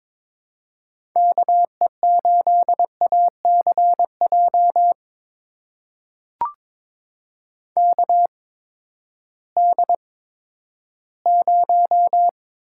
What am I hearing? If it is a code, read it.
Morse code, heard as a keyed tone, about 22 wpm, spaced as for 12 wpm, 700 Hz. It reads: KE8ACJ KD0
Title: USA Call Signs - 6 Characters - 8x Character Spacing - 22wpm